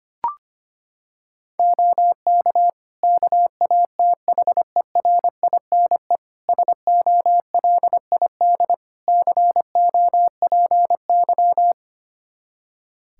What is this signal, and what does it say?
Morse code, heard as a keyed tone, about 25 wpm, 700 Hz. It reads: OK KATHERINE SOLID COPY
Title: Practice QSO - Exchange 3 - More Info - 25wpm